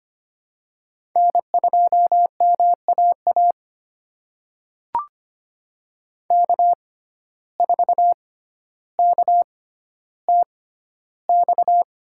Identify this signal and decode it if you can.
Morse code, heard as a keyed tone, about 25 wpm, spaced as for 13 wpm, 700 Hz. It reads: N2MAA K4KTX
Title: USA Call Signs - 6x Character Spacing - 25wpm